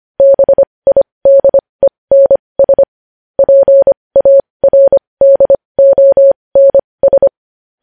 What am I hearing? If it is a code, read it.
Morse code, heard as a keyed tone, about 25 wpm, spaced as for 20 wpm, 550 Hz. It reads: BIDENS PARDONS
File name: podcast / News headlines-1x-25x20